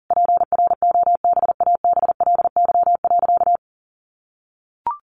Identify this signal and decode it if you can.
Morse code, heard as a keyed tone, about 40 wpm, 700 Hz. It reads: PROBABLY.
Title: Sentences from Top 700 Words 40wpm